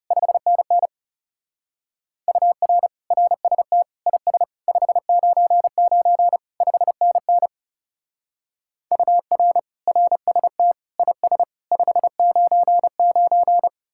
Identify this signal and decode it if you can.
Morse code, heard as a keyed tone, about 38 wpm, 700 Hz. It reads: 5NN UR RST IS 599 5NN UR RST IS 599